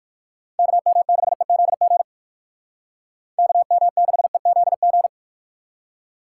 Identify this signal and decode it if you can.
Morse code, heard as a keyed tone, about 51 wpm, 700 Hz. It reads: KM6EZG KM6EZG
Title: Speed Racing - USA 2x3 Call Signs 45wpm